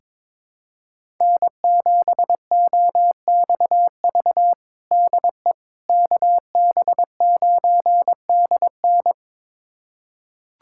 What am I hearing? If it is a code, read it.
Morse code, heard as a keyed tone, about 22 wpm, 700 Hz. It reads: N7OXV DE KB9DN
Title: USA Callsign DE Callsign - 5 Characters - 22wpm